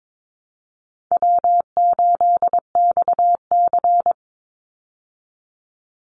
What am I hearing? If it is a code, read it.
Morse code, heard as a keyed tone, about 22 wpm, 700 Hz. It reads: W8XC
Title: Easy Call Signs Repeated 2x - 22wpm